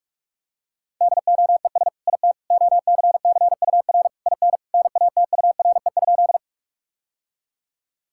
Morse code, heard as a keyed tone, about 46 wpm, 700 Hz. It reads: DOES IT OCCUR IN NATURE?
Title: Sentences from Top 1000 Words - Review Part 6 - 45wpm